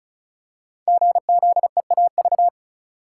Morse code, heard as a keyed tone, about 35 wpm, 700 Hz. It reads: GZEAV
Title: Code Groups - 5 Letter:Numbers - 35wpm